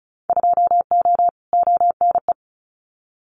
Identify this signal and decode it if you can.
Morse code, heard as a keyed tone, about 34 wpm, 700 Hz. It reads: 2O ONE
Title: Winter Field Day Exchanges Encore 35wpm